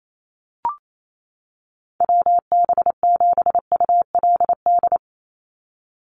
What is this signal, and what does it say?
Morse code, heard as a keyed tone, about 28 wpm, 700 Hz. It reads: WB7ULD
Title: NA SOTA Chaser Callsigns 28wpm